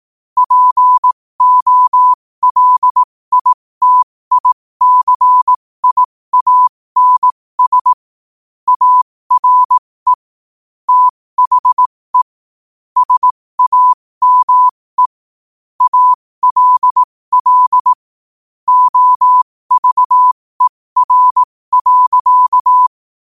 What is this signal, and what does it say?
Morse code, heard as a keyed tone, about 18 wpm, spaced as for 15 wpm, 1000 Hz. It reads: POLITICIANS ARE THE SAME ALL OVER.
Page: New quotes every day in morse code at 15 Words per minute.